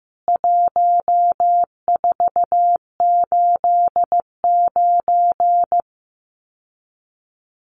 Morse code, heard as a keyed tone, about 15 wpm, 700 Hz. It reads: 1489
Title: CWT Exchange 15wpm